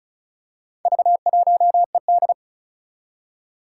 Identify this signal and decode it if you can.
Morse code, heard as a keyed tone, about 36 wpm, 700 Hz. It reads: V1ED
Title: Four Challenging Letter-Number Combinations 35wpm